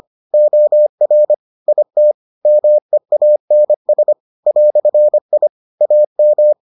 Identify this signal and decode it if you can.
Morse code, heard as a keyed tone, about 25 wpm, 600 Hz. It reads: OR IT MEANS "I AM